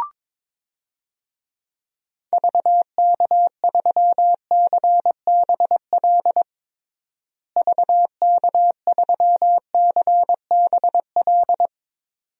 Morse code, heard as a keyed tone, about 22 wpm, 700 Hz. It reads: VK3CBL VK3CBL
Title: DXCC-100 Call Signs - 6 Characters - Repeated 2x - 22wpm